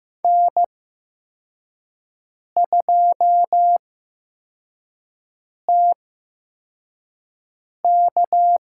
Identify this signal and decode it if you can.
Morse code, heard as a keyed tone, about 15 wpm, spaced as for 4 wpm, 700 Hz. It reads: N2TK
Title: USA Call Signs - 5 Characters - 8x Character Spacing - 15wpm